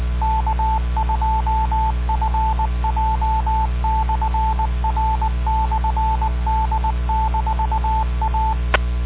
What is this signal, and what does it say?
Morse code, heard as a keyed tone, about 19 wpm, 880 Hz, with radio noise. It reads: K2FJ/R/D-A